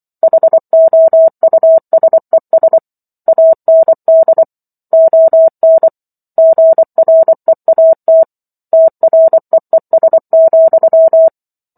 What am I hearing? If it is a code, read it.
Morse code, heard as a keyed tone, about 24 wpm, 650 Hz. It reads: HOUSES AND ON GREAT TREES,